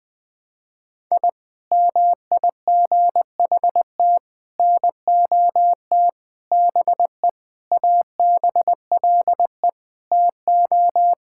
Morse code, heard as a keyed tone, about 20 wpm, 700 Hz. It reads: I MIGHT NOT BE ABLE TO
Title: Sentences with 4-Word Phrases - Spoken Beginning - 20wpm